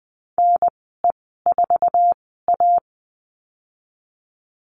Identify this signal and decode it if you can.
Morse code, heard as a keyed tone, about 20 wpm, spaced as for 15 wpm, 700 Hz. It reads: NE4A
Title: 069 Lesson 12g - Letter d - Call Signs 20wpm